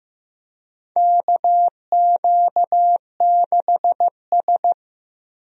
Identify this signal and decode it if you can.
Morse code, heard as a keyed tone, about 15 wpm, 700 Hz. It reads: KQ6S
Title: USA Call Signs - 4 Characters - 15wpm